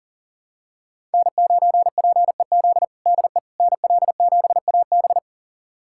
Morse code, heard as a keyed tone, about 40 wpm, 700 Hz. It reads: N9PEZ DE NL7AB
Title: USA Callsign DE Callsign - 5 Characters - 40wpm